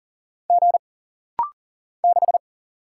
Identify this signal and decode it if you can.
Morse code, heard as a keyed tone, about 39 wpm, 700 Hz. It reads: G 6
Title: Single Letter-Number - Rapid-Fire 40wpm